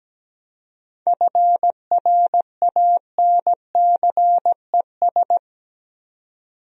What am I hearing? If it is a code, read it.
Morse code, heard as a keyed tone, about 17 wpm, 700 Hz. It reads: FRANCES
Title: USA Common Names - 17wpm